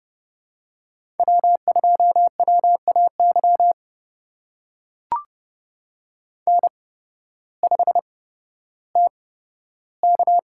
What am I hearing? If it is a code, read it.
Morse code, heard as a keyed tone, about 30 wpm, spaced as for 14 wpm, 700 Hz. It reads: W2WAY N5TK
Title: USA Call Signs - 8x Character Spacing - 30wpm